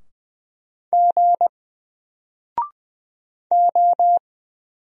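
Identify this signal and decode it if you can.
Morse code, heard as a keyed tone, about 20 wpm, 700 Hz. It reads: G O
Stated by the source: Single letter and number Morse drill at 20 wpm, from N4LCD.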